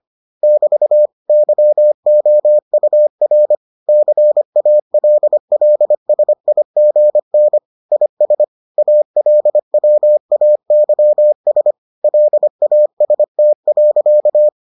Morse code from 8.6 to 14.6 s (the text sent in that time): ALWAYS LAST.